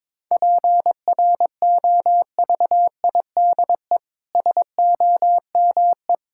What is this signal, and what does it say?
Morse code, heard as a keyed tone, about 22 wpm, 700 Hz. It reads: PROVIDE SOME
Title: Sentences from Top 1000 Words - Review Part 6 - 22wpm